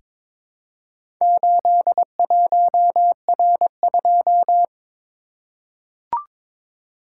Morse code, heard as a keyed tone, about 22 wpm, 700 Hz. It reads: 81R2
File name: practice / 22wpm / Four Letter-Number Combinations - 22wpm